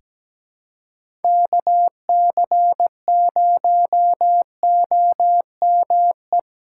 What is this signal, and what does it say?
Morse code, heard as a keyed tone, about 17 wpm, 700 Hz. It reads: KC0OME
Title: USA Call Signs - 6 Characters - 4x Character Spacing - 17wpm